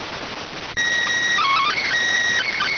SSTV (slow-scan television), PD 290.